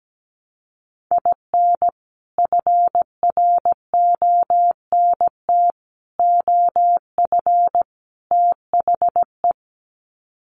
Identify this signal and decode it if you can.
Morse code, heard as a keyed tone, about 17 wpm, 700 Hz. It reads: IN FRONT OF THE
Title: Sets of 4 Words 17wpm